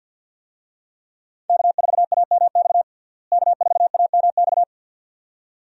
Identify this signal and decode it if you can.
Morse code, heard as a keyed tone, about 51 wpm, 700 Hz. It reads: K4AMX K4AMX
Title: USA Call Signs - 5 Characters - Repeated 2x - 50wpm